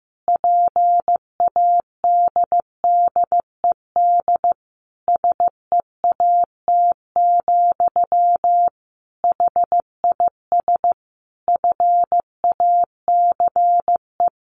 Morse code, heard as a keyed tone, about 15 wpm, 700 Hz. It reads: PADDED SEAT, HIS FACE
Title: A Sound Of Thunder 15wpm